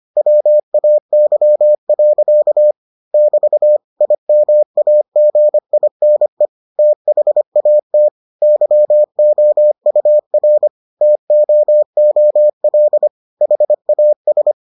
Morse code, heard as a keyed tone, about 25 wpm, 600 Hz. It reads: WAY. <BT> IMAGINE THAT YOUR TOOL HAS